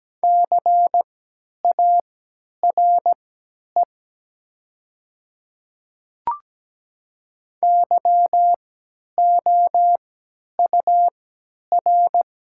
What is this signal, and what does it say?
Morse code, heard as a keyed tone, about 17 wpm, spaced as for 10 wpm, 700 Hz. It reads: CARE YOUR
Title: 3x Character Spacing - Top 300 Words - 17wpm